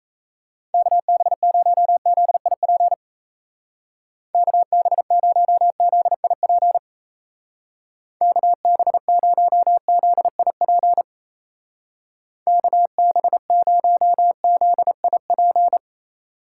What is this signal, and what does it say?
Morse code, heard as a keyed tone, about 36 wpm, 700 Hz. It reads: KB0ZIP KB0ZIP KB0ZIP KB0ZIP